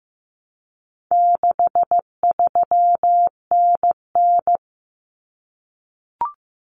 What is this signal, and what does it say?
Morse code, heard as a keyed tone, about 15 wpm, 700 Hz. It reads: 63NN
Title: Common Cut Numbers - 4-Digit 15wpm